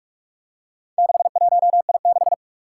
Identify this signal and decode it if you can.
Morse code, heard as a keyed tone, about 44 wpm, 700 Hz. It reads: 61I6
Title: Four Challenging Letter-Number Combinations 45wpm